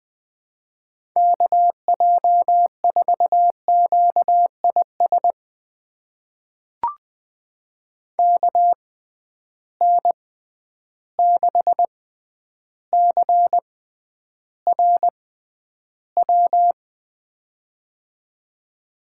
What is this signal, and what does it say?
Morse code, heard as a keyed tone, about 20 wpm, spaced as for 10 wpm, 700 Hz. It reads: KJ4QIS KN6CRW
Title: USA Call Signs - 6 Characters - 6x Character Spacing - 20wpm